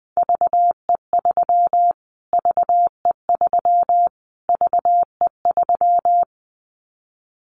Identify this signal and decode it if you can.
Morse code, heard as a keyed tone, about 20 wpm, 700 Hz. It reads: VE3 VE3 VE3
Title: Callsign Prefixes for Top 25 DXCC - Repeated 3x - 20wpm